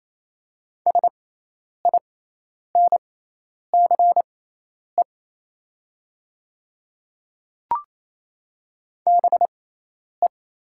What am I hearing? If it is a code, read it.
Morse code, heard as a keyed tone, about 27 wpm, spaced as for 10 wpm, 700 Hz. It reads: SINCE BE
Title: 6x Character Spacing - Top 500 Words - 28wpm